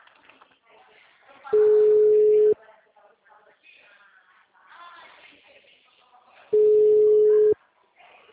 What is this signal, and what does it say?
Ringback tone (Europe, 425 Hz), with speech.